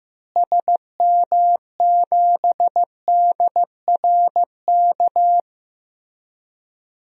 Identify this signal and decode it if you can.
Morse code, heard as a keyed tone, about 15 wpm, 700 Hz. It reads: SM7DRK
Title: DXCC-100 Call Signs - 6 Characters - Repeated 2x - 15wpm